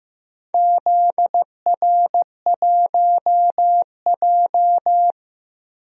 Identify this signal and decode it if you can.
Morse code, heard as a keyed tone, about 15 wpm, 700 Hz. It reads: ZR1J
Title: DXCC-100 Call Signs - 4 Characters - Repeated 2x - 15wpm